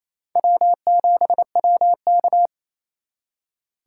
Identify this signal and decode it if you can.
Morse code, heard as a keyed tone, about 34 wpm, 700 Hz. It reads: W7WK
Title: Speed Racing - USA 1x2 Call Signs 28wpm